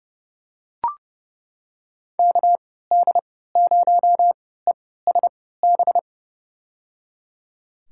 Morse code, heard as a keyed tone, about 30 wpm, spaced as for 17 wpm, 700 Hz. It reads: KD0ESB
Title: USA Call Signs - 6 Characters - 3x Character Spacing - 30wpm